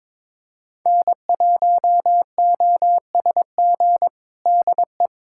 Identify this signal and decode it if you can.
Morse code, heard as a keyed tone, about 21 wpm, 700 Hz. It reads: N1OSG DE
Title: USA Callsign DE Callsign - 5 Characters - 22wpm